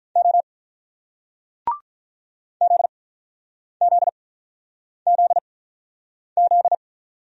Morse code, heard as a keyed tone, about 48 wpm, spaced as for 9 wpm, 700 Hz. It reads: O ZZZZ